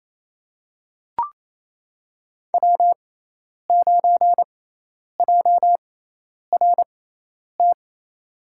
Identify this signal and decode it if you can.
Morse code, heard as a keyed tone, about 28 wpm, spaced as for 10 wpm, 700 Hz. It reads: W9JRT
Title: USA Call Signs - 5 Characters - 6x Character Spacing - 28wpm